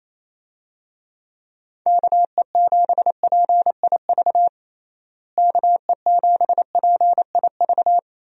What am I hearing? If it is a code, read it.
Morse code, heard as a keyed tone, about 28 wpm, 700 Hz. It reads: KE7PIV KE7PIV
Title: USA Call Signs - 6 Characters - Repeated 3x - 28wpm